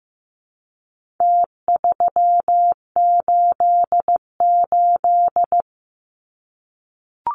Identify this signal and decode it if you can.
Morse code, heard as a keyed tone, about 15 wpm, 700 Hz. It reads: T388
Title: Numbers - Contest Style - 4-Digit 15wpm